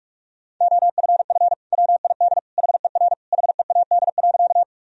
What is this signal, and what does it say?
Morse code, heard as a keyed tone, about 47 wpm, 700 Hz. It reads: OFF WID HER HEAD.
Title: Sentences from Top 300 Words and Abbreviations 45wpm